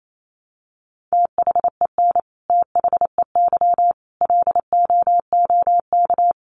Morse code, heard as a keyed tone, about 28 wpm, 700 Hz. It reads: THEN THEY LOOK
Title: Sentences from Top 1300 Words 28wpm